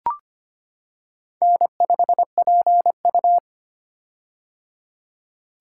Morse code, heard as a keyed tone, about 25 wpm, 700 Hz. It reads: N5PU
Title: K1USN SST QSO Practice - 25wpm